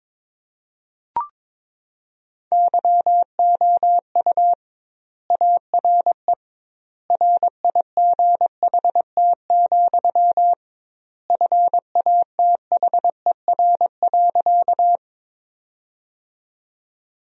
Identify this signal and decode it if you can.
Morse code, heard as a keyed tone, about 22 wpm, 700 Hz. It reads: YOU ARE RIGHT, FATHER.
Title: 2x Word-Spacing - Sentences from Top 300 Words - 22wpm